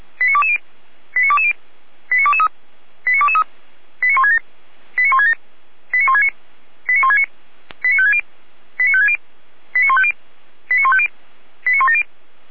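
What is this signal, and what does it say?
Five-tone selective call (ZVEI), heard as quick runs of five tones, tones of about 70 ms. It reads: 89220 89220 89223 89223 89167 89167 89169 89169 89560 89560 89150 89150 89170
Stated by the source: ZVEI five-tone selective call received on an SDR.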